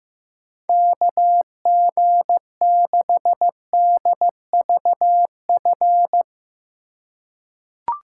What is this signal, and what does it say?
Morse code, heard as a keyed tone, about 15 wpm, 700 Hz. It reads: KG6DVF
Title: USA Call Signs - 6 Characters - 4x Character Spacing - 15wpm